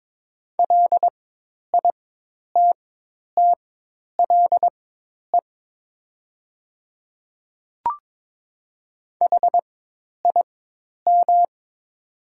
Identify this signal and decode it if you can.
Morse code, heard as a keyed tone, about 22 wpm, spaced as for 10 wpm, 700 Hz. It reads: LITTLE HIM
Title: 4x Character Spacing - Top 300 Words - 22wpm